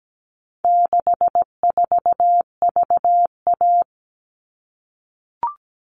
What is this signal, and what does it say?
Morse code, heard as a keyed tone, about 17 wpm, 700 Hz. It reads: 64VA
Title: All Cut Numbers - 4-Digit 17wpm